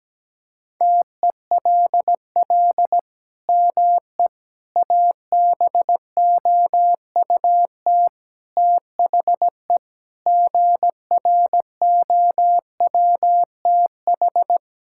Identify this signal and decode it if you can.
Morse code, heard as a keyed tone, about 17 wpm, 700 Hz. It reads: TELL ME ABOUT THE GROWTH
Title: Sentences from Top 1000 Words - Review Part 5 - 17wpm